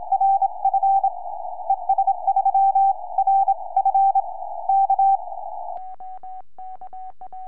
Morse code, heard as a keyed tone, about 25 wpm, 780 Hz. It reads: RF ES3RF K
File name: es3rf 4m